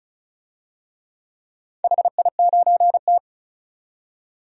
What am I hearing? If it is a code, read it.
Morse code, heard as a keyed tone, about 34 wpm, 700 Hz. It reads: HI9T
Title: DXCC-100 Call Signs - 4 Characters - 35wpm